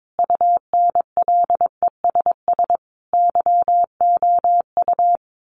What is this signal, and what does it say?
Morse code, heard as a keyed tone, about 22 wpm, 700 Hz. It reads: UNLESS YOU
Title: Sentences from Top 1500 Words 22wpm